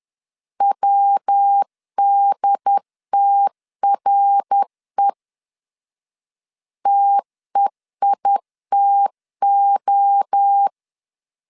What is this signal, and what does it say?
Morse code, heard as a keyed tone, about 10 wpm, 800 Hz. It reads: WDTRE TEITO
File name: CW / DET / 1 3 1500C S B 10wpm 3335